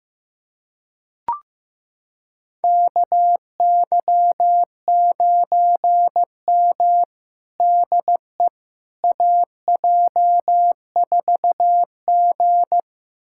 Morse code, heard as a keyed tone, about 15 wpm, 700 Hz. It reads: KY9M DE AJ4G
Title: USA Callsign DE Callsign - 4 Characters - 15wpm